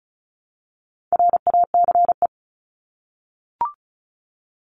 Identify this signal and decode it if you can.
Morse code, heard as a keyed tone, about 34 wpm, 700 Hz. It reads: RACE
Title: Top 1501-1750 Words 35wpm